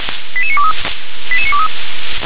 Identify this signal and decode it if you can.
Five-tone selective call (ZVEI), heard as quick runs of five tones, tones of about 70 ms. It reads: 88923 88923